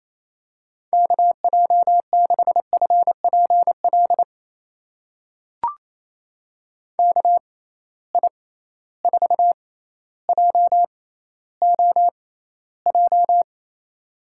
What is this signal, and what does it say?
Morse code, heard as a keyed tone, about 28 wpm, spaced as for 14 wpm, 700 Hz. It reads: KJ6FPL KI4JOJ
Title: USA Call Signs - 6 Characters - 6x Character Spacing - 28wpm